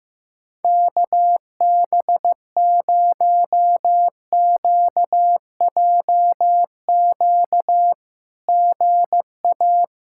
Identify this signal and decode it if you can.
Morse code, heard as a keyed tone, about 15 wpm, 700 Hz. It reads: KB0QJQ GA
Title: SOTA QSO Practice 15wpm